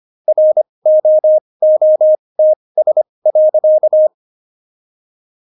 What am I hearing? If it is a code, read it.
Morse code, heard as a keyed tone, about 25 wpm, spaced as for 20 wpm, 620 Hz. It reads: ROOTS.
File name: Jokes-25-char-20-eff